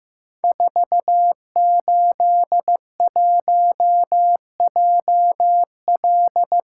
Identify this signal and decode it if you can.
Morse code, heard as a keyed tone, about 15 wpm, 700 Hz. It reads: 481JL
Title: Code Groups - 5 Letter:Numbers - 15wpm